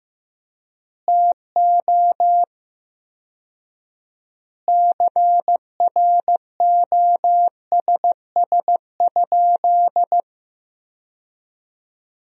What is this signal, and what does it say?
Morse code, heard as a keyed tone, about 16 wpm, 700 Hz. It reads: TO CROSS?
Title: 4x Word-Spacing - Sentences from Top 300 Words - 15wpm